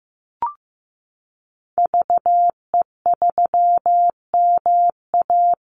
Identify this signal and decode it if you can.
Morse code, heard as a keyed tone, about 15 wpm, 700 Hz. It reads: VE3MA
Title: Call Signs Repeated 3x - 15wpm